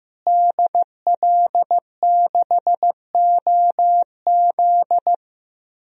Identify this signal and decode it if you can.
Morse code, heard as a keyed tone, about 15 wpm, 700 Hz. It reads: DL6OZ
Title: DXCC-100 Call Signs - 5 Characters - Repeated 2x - 15wpm